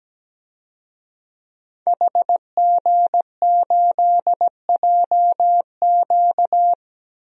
Morse code, heard as a keyed tone, about 17 wpm, 700 Hz. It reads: HG8JQ
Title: DXCC-100 Call Signs - 5 Characters - Repeated 3x - 17wpm